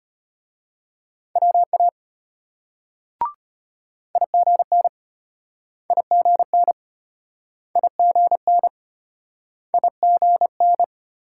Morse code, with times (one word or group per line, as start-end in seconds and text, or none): 1.4-1.9 WA
4.2-4.9 IGN
5.9-6.7 IGN
7.8-8.7 IGN
9.7-10.8 IGN